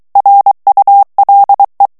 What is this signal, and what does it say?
Morse code, heard as a keyed tone, about 23 wpm, 800 Hz. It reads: RULE
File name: ANGLAIS texte clair 3 vitesse 22 mots minute